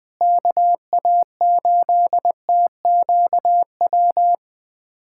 Morse code, heard as a keyed tone, about 21 wpm, 700 Hz. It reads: KA8TQW
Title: USA Callsign DE Callsign 20wpm